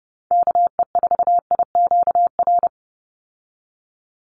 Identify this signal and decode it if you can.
Morse code, heard as a keyed tone, about 30 wpm, 700 Hz. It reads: KE4IQR
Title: USA Call Signs 30wpm